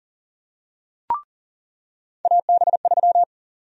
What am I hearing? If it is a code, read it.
Morse code, heard as a keyed tone, about 40 wpm, 700 Hz. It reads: AB3
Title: USA Call Sign Prefixes - 40wpm